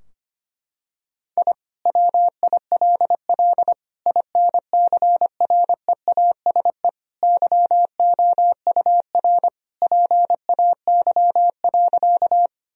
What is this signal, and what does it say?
Morse code, heard as a keyed tone, about 25 wpm, 700 Hz. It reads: I WILL INCREASE YOUR PAY.
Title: Sentences from Top 1000 Words - Review Part 4 - 25wpm